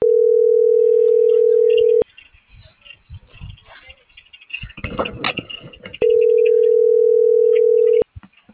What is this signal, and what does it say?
Ringback tone (North America), 440 + 480 Hz.